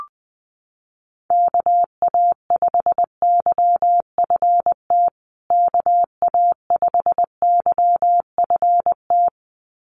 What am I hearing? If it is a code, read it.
Morse code, heard as a keyed tone, about 20 wpm, 700 Hz. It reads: KA5YFT KA5YFT
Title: USA Call Signs Repeated 2x 20wpm